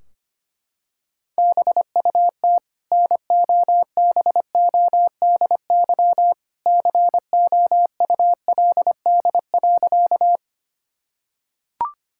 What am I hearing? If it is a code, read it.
Morse code, heard as a keyed tone, about 25 wpm, 700 Hz. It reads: BUT NOBODY COULD.
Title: Sentences from Top 1600 Words 25wpm